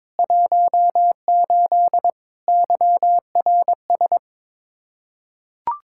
Morse code, heard as a keyed tone, about 21 wpm, 700 Hz. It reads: 18 YRS
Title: Practice QSO - Exchange 3 - More Info - 22wpm